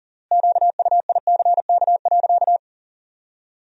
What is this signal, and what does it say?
Morse code, heard as a keyed tone, about 40 wpm, 700 Hz. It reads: QUICK.
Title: 3x Word-Spacing - Sentences from Top 500 Words - 40wpm